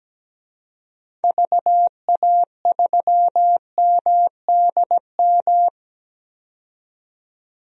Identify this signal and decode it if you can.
Morse code, heard as a keyed tone, about 17 wpm, 700 Hz. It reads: VA3MDM
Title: DXCC-100 Call Signs - 6 Characters - 17wpm